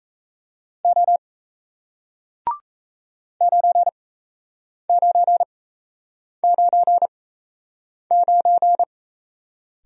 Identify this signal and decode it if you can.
Morse code, heard as a keyed tone, about 38 wpm, spaced as for 8 wpm, 700 Hz. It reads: O 9999